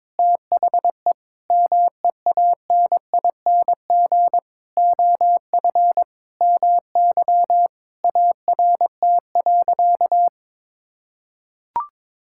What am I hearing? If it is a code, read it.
Morse code, heard as a keyed tone, about 22 wpm, 700 Hz. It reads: THE MEANING OF MY ART.
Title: Sentences from Top 1000 Words - Review Part 6 - 22wpm